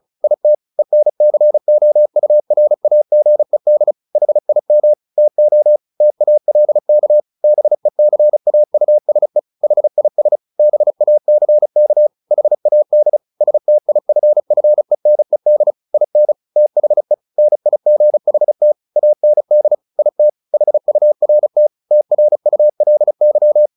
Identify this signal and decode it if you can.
Morse code, heard as a keyed tone, about 34 wpm, 600 Hz. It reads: IT ENCOURAGED HIM TO TALK BECAUSE HIS BACK HAD STIFFENED IN THE NIGHT AND IT HURT TRULY
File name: the old man and the sea / the old man and the sea 1 35wpm